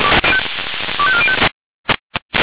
Five-tone selective call (ZVEI), heard as quick runs of five tones, tones of about 70 ms. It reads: 25405